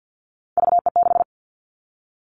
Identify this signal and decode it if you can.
Morse code, heard as a keyed tone, about 50 wpm, 700 Hz. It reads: VE6
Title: Callsign Prefixes for Top 100 DXCC - 50wpm